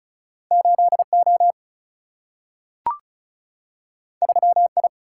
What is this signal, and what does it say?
Morse code, heard as a keyed tone, about 36 wpm, 700 Hz. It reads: 8O 3I
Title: Single-Letter Single-Number 35wpm